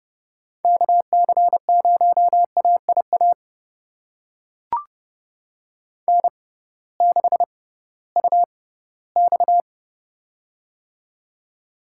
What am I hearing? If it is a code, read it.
Morse code, heard as a keyed tone, about 30 wpm, spaced as for 17 wpm, 700 Hz. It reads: KC0AIA N6UX
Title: USA Call Signs - 6x Character Spacing - 30wpm